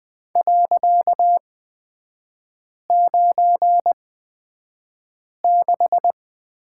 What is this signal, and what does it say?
Morse code, heard as a keyed tone, about 20 wpm, spaced as for 5 wpm, 700 Hz. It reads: .96
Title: 255 Lesson 41e - Period - Three Characters 20wpm